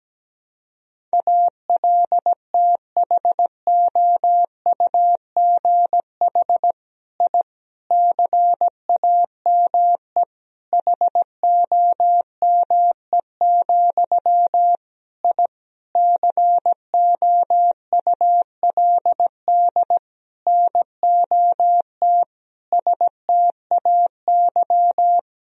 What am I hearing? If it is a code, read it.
Morse code, heard as a keyed tone, about 17 wpm, 700 Hz. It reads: ALTHOUGH I CAME HOME, I COULD NOT STAY
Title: Sentences from Top 1000 Words - Review Part 3 - 17wpm